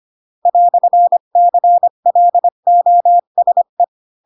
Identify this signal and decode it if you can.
Morse code, heard as a keyed tone, about 25 wpm, 700 Hz. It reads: "CLOSE
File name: Pinocchio / Pinocchio ch8 20 25 ik1hge